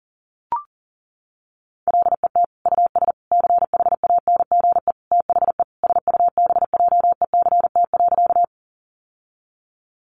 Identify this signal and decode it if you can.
Morse code, heard as a keyed tone, about 40 wpm, 700 Hz. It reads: LET US CHANGE THE SUBJECT.
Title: Sentences from Top 800 Words 40wpm